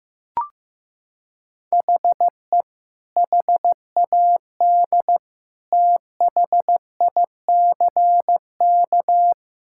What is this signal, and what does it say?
Morse code, heard as a keyed tone, about 16 wpm, 700 Hz. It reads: HE HAD THICK
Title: Sentences from Top 1600 Words 15wpm